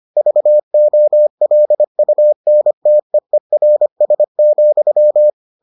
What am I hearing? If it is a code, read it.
Morse code, heard as a keyed tone, about 25 wpm, 600 Hz. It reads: VOLUNTEERS,